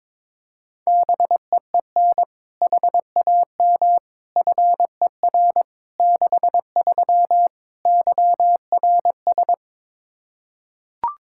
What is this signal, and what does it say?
Morse code, heard as a keyed tone, about 22 wpm, 700 Hz. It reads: BEEN HAM FER 63 YRS
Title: Practice QSO - Exchange 3 - More Info - 22wpm